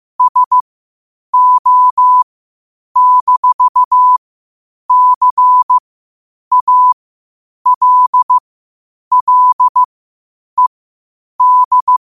Morse code, heard as a keyed tone, about 15 wpm, spaced as for 8 wpm, 1000 Hz. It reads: SO-CALLED